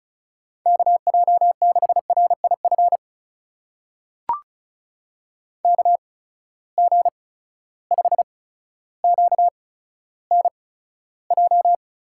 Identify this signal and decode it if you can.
Morse code, heard as a keyed tone, about 36 wpm, spaced as for 15 wpm, 700 Hz. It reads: KJ6RIF KG5QNJ
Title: USA Call Signs - 8x Character Spacing - 35wpm